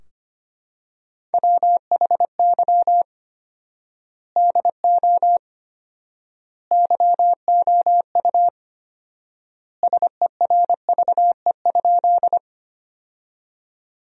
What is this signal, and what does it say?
Morse code, heard as a keyed tone, about 25 wpm, 700 Hz. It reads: WHY DO YOU SERVE?
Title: 4x Word-Spacing - Sentences from Top 400 Words - 25wpm